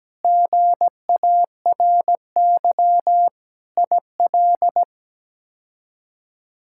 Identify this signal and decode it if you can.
Morse code, heard as a keyed tone, about 17 wpm, 700 Hz. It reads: GARY IL
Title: K1USN SST QSO Practice - 17wpm